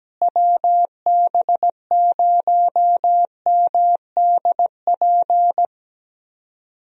Morse code, heard as a keyed tone, about 17 wpm, 700 Hz. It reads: WB0MDP
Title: USA Call Signs - 6 Characters - Repeated 3x - 17wpm